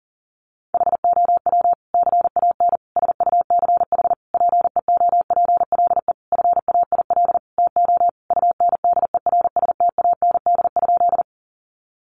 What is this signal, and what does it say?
Morse code, heard as a keyed tone, about 40 wpm, 700 Hz. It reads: HOW CAN SUCH PEOPLE FAIL TO UNDERSTAND?
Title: Sentences from Top 1100 Words 40wpm